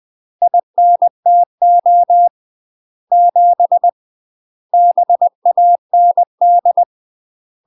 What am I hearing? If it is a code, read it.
Morse code, heard as a keyed tone, about 20 wpm, 700 Hz. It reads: INTO 7 BAND
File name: UK MASTER QSOs TRAINING / 20WPM / UK MASTER QSOs 0068